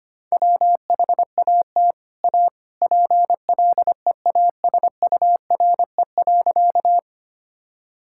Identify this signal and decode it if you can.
Morse code, heard as a keyed tone, about 25 wpm, 700 Hz. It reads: WHAT A PLEASURE.
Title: Sentences from Top 1600 Words 25wpm